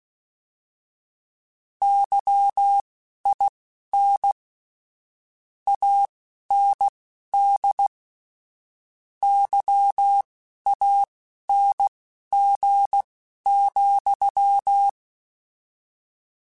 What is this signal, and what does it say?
Morse code, heard as a keyed tone, about 16 wpm, spaced as for 11 wpm, 800 Hz. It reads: YIN AND YANG,
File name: Misc / Misc-mp3s / Word Pairs @10